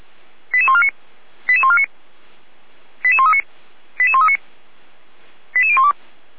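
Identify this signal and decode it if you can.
Five-tone selective call (ZVEI), heard as quick runs of five tones, tones of about 70 ms. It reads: 88149 88149 88139 88139 88012
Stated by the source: Received on an SDR; ZVEI five-tone selective call, code 88190.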